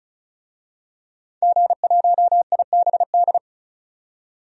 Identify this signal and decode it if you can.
Morse code, heard as a keyed tone, about 34 wpm, 700 Hz. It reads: G1IBD
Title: DXCC-100 Call Signs - 5 Characters - 35wpm